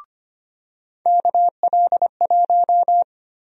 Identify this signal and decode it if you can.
Morse code, heard as a keyed tone, about 25 wpm, 700 Hz. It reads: KL1
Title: USA Call Sign Prefixes - 25wpm